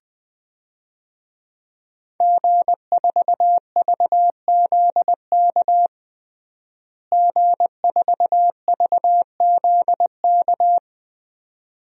Morse code, heard as a keyed tone, about 20 wpm, 700 Hz. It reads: G4VZK G4VZK
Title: DXCC-100 Call Signs - 5 Characters - Repeated 2x - 20wpm